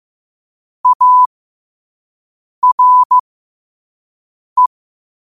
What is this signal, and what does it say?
Morse code, heard as a keyed tone, about 15 wpm, spaced as for 5 wpm, 1000 Hz. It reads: ARE
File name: qotdfiles / 5wpm